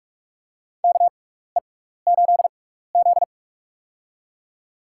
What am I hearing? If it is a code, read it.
Morse code, heard as a keyed tone, about 46 wpm, spaced as for 16 wpm, 700 Hz. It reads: KE8Z
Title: USA Call Signs - 4 Characters - 6x Character Spacing - 45wpm